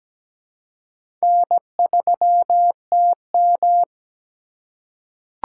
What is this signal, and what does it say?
Morse code, heard as a keyed tone, about 17 wpm, 700 Hz. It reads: N3TM
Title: USA Call Signs - 4x Character Spacing - 17wpm